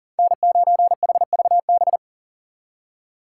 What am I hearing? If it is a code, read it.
Morse code, heard as a keyed tone, about 40 wpm, 700 Hz. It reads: N9HVB
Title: USA Call Signs - 5 Characters - 6x Character Spacing - 40wpm